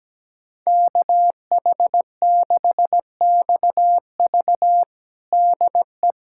Morse code, heard as a keyed tone, about 17 wpm, 700 Hz. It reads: KH6XV DE
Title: USA Callsign DE Callsign - 5 Characters - 17wpm